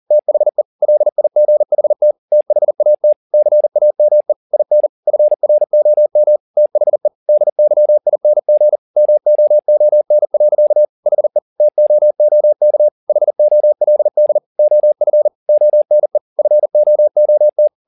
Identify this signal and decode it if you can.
Morse code, heard as a keyed tone, about 40 wpm, 600 Hz. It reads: THE LIGHT THAT CAME IN FROM THE DYING MOON. HE TOOK HOLD OF ONE FOOT